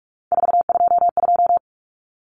Morse code, heard as a keyed tone, about 44 wpm, 700 Hz. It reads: 422
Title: Common Cut Numbers - 3-Digit 45wpm